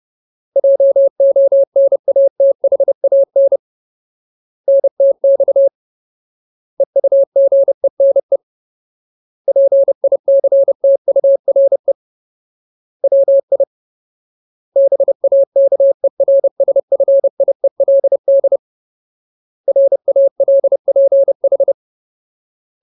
Morse code, heard as a keyed tone, about 30 wpm, 550 Hz. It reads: JONATHAN NTX EUGENE PICTURE WI BAKERSFIELD RALPH
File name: Ultimate Word List / 30WPM Ultimate